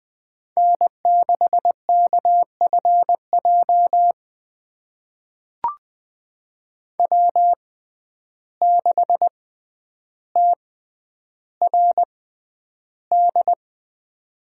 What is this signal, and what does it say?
Morse code, heard as a keyed tone, about 20 wpm, spaced as for 10 wpm, 700 Hz. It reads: N6KFJ W6TRD
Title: USA Call Signs - 5 Characters - 6x Character Spacing - 20wpm